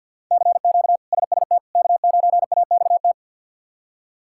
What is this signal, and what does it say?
Morse code, heard as a keyed tone, about 47 wpm, 700 Hz. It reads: CQ SST K9AXT
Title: K1USN SST QSO Practice - 50wpm